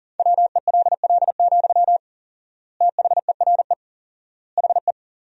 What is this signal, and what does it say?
Morse code, heard as a keyed tone, about 37 wpm, 700 Hz. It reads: WELL, THERE HE
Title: 4x Word-Spacing - Sentences from Top 500 Words - 40wpm